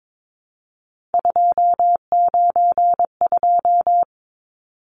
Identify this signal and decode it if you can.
Morse code, heard as a keyed tone, about 22 wpm, 700 Hz. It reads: 292
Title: Numbers - 3-Digit 22wpm